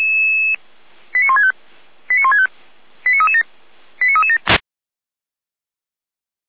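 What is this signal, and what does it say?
Five-tone selective call (ZVEI), heard as quick runs of five tones, tones of about 70 ms. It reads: 89165 89165 89337 89337
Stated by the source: ZVEI five-tone selective call received via SDR, from the Feuerwehr Orlamünde audio archive.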